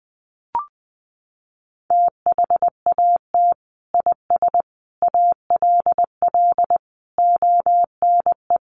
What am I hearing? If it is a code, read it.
Morse code, heard as a keyed tone, about 20 wpm, 700 Hz. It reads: THAT IS ALL ONE
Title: Sentences from Top 200 Words 20wpm